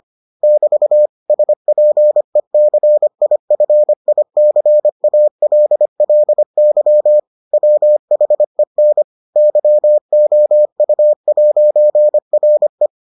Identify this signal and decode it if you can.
Morse code, heard as a keyed tone, about 25 wpm, 600 Hz. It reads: <BT> SPECIFICALLY WHEN YOU'RE